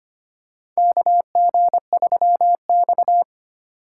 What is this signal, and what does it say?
Morse code, heard as a keyed tone, about 25 wpm, 700 Hz. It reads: KG3X
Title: USA Call Signs - 4 Characters - 4x Character Spacing - 25wpm